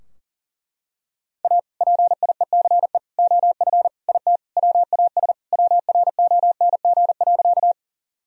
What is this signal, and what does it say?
Morse code, heard as a keyed tone, about 40 wpm, 700 Hz. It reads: A PIECE OF IT WAS WRONG.
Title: Sentences from Top 1000 Words - Review Part 2 - 40wpm